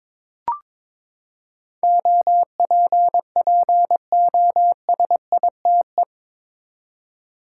Morse code, heard as a keyed tone, about 22 wpm, 700 Hz. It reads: OPPOSITE
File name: practice / 22wpm / Eight Letter Words - 22wpm